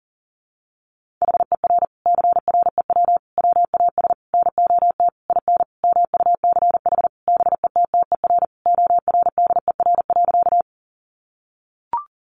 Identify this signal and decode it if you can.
Morse code, heard as a keyed tone, about 40 wpm, 700 Hz. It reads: HER CREW WAS NOT IN MUCH BETTER ORDER.
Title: Sentences from Top 1500 Words 40wpm